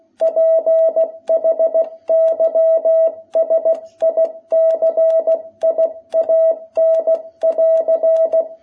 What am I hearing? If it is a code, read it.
Morse code, heard as a keyed tone, about 16 wpm, 660 Hz. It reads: PHYSICIAN<AR>
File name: jh8cbh / 2021cw